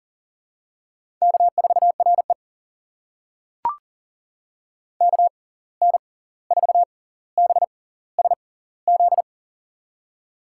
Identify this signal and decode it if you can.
Morse code, heard as a keyed tone, about 40 wpm, spaced as for 18 wpm, 700 Hz. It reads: K4RE KN4BSZ
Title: USA Call Signs - 6x Character Spacing - 40wpm